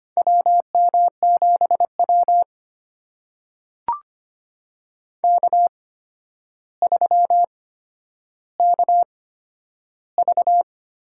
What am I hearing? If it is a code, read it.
Morse code, heard as a keyed tone, about 25 wpm, spaced as for 11 wpm, 700 Hz. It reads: WM7W K3KV
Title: USA Call Signs - 4 Characters - 8x Character Spacing - 25wpm